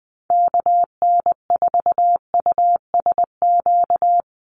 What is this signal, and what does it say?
Morse code, heard as a keyed tone, about 20 wpm, 700 Hz. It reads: KN4USQ
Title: USA Call Signs Repeated 3x 20wpm